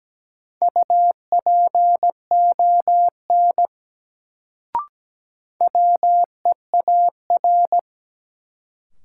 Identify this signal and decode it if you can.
Morse code, heard as a keyed tone, about 17 wpm, 700 Hz. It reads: UPON WEAR
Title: Sentences from Top 1000 Words - Review Part 6 - New Word Drill - 17wpm